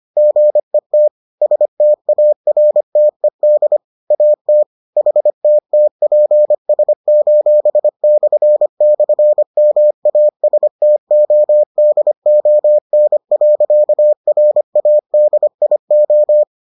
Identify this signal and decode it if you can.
Morse code, heard as a keyed tone, about 25 wpm, 600 Hz. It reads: GET STARTED AT HTTPS://MASTODON.RADIO